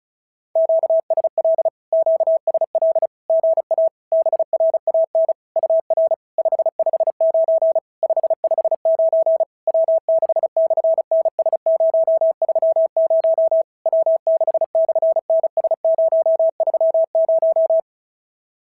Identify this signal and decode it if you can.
Morse code, heard as a keyed tone, about 35 wpm, 650 Hz. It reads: QSL QSL GA BRAN UR 559 559 W6/NS030 W6/NS030
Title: SOTA QSO Practice 35wpm